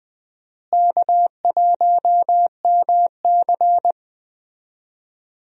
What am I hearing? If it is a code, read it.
Morse code, heard as a keyed tone, about 20 wpm, 700 Hz. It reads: K1MC
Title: USA Call Signs - 4 Characters - Repeated 3x - 20wpm